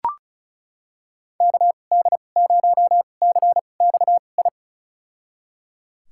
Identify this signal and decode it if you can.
Morse code, heard as a keyed tone, about 34 wpm, spaced as for 25 wpm, 700 Hz. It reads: KD0CXI
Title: USA Call Signs - 6 Characters - 2x Character Spacing - 35wpm